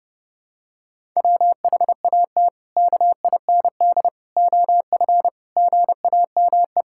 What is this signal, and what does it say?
Morse code, heard as a keyed tone, about 30 wpm, 700 Hz. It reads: WHAT KIND OF GAME
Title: Sentences from Top 1000 Words - Review Part 6 - 30wpm